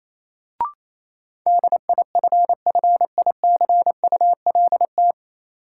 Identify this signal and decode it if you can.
Morse code, heard as a keyed tone, about 28 wpm, 700 Hz. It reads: DIFFICULT
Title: Sets of 4 Words - New Word Drill - 28wpm